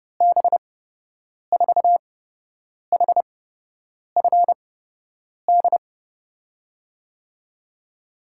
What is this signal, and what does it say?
Morse code, heard as a keyed tone, about 30 wpm, spaced as for 8 wpm, 700 Hz. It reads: B4HFD
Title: USA Call Signs - 6 Characters - 8x Character Spacing - 30wpm